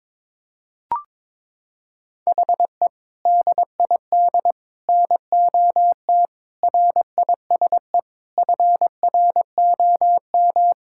Morse code, heard as a keyed tone, about 22 wpm, 700 Hz. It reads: HE DID NOT RISE FROM
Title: Sentences from Top 1000 Words - Review Part 6 - 22wpm